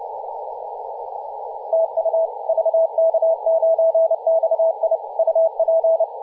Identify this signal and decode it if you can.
Morse code, heard as a keyed tone, about 31 wpm, 650 Hz, with radio noise. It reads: TU VK9XI UP
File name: posts / 2017-oct-04-0559z-vk9xi-21-032mhz-cw